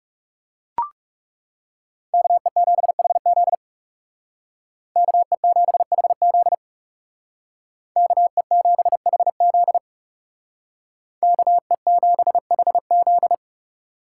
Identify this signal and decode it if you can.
Morse code, heard as a keyed tone, about 38 wpm, 700 Hz. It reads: KE7HZ KE7HZ KE7HZ KE7HZ